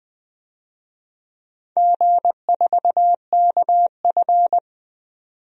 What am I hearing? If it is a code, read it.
Morse code, heard as a keyed tone, about 20 wpm, 700 Hz. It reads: G4KF
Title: DXCC-100 Call Signs - 4 Characters - 20wpm